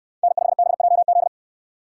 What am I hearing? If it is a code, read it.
Morse code, heard as a keyed tone, about 63 wpm, 700 Hz. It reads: N5BJZ